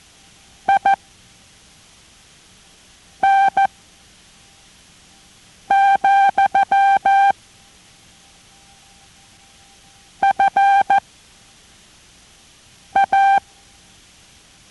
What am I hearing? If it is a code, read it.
Morse code, heard as a keyed tone, about 14 wpm, 790 Hz. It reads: I N , F A